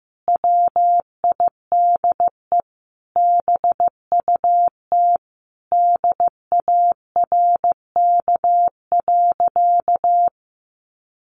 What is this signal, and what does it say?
Morse code, heard as a keyed tone, about 15 wpm, 700 Hz. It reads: WIDE BUT DARK.
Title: Sentences from Top 1400 Words 15wpm